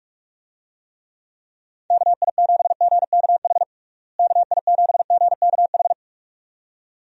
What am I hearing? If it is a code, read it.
Morse code, heard as a keyed tone, about 41 wpm, 700 Hz. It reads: KI7GKH KI7GKH
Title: USA Call Signs - 6 Characters - Repeated 2x - 45wpm